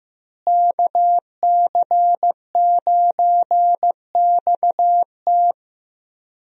Morse code, heard as a keyed tone, about 15 wpm, 700 Hz. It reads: KC9XT
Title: USA Call Signs - 5 Characters - Repeated 2x - 15wpm